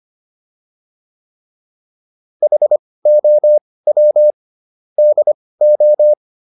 Morse code, heard as a keyed tone, about 25 wpm, spaced as for 18 wpm, 600 Hz. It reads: HOW DO
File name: WD204 18